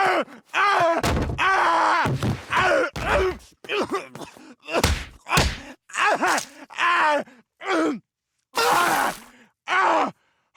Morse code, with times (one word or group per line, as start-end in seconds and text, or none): none